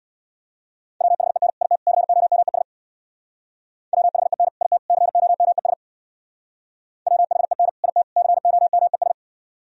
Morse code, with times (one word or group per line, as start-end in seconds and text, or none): 1.0-1.5 WHEN
1.6-1.8 IT
1.9-2.6 COMES
3.9-4.5 WHEN
4.6-4.8 IT
4.9-5.7 COMES
7.1-7.7 WHEN
7.8-8.0 IT
8.2-9.1 COMES